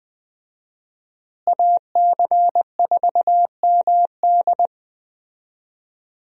Morse code, heard as a keyed tone, about 21 wpm, 700 Hz. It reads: AC4MD
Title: USA Call Signs - 5 Characters - Repeated 3x - 20wpm